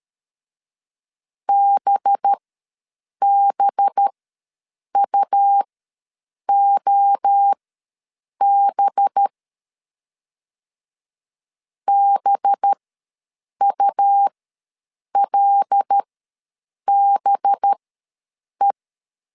Morse code, heard as a keyed tone, about 13 wpm, spaced as for 7 wpm, 800 Hz. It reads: BBUOB BULBE
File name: LEZIONI / 5 LBVC / 5 LB ws 12wpm 3034